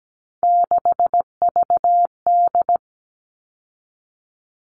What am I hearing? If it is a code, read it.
Morse code, heard as a keyed tone, about 17 wpm, 700 Hz. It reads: 6VD
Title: All Cut Numbers - 3-Digit 17wpm